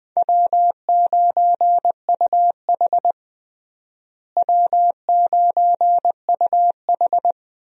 Morse code, heard as a keyed tone, about 20 wpm, 700 Hz. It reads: W9UH W9UH
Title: USA Call Signs - 4 Characters - Repeated 2x - 20wpm